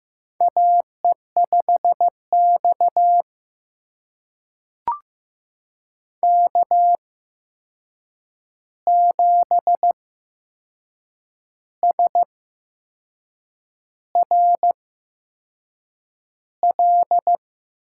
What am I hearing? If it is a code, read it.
Morse code, heard as a keyed tone, about 15 wpm, spaced as for 6 wpm, 700 Hz. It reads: AE5X K7SRL
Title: USA Call Signs - 8x Character Spacing - 15wpm